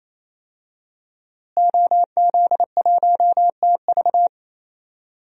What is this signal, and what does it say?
Morse code, heard as a keyed tone, about 27 wpm, 700 Hz. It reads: OZ1TV
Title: DXCC-100 Call Signs - 5 Characters - Repeated 3x - 28wpm